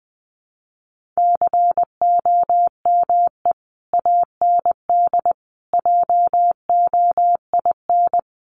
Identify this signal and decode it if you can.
Morse code, heard as a keyed tone, about 20 wpm, 700 Hz. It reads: COME AND JOIN
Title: Sentences from Top 1400 Words 20wpm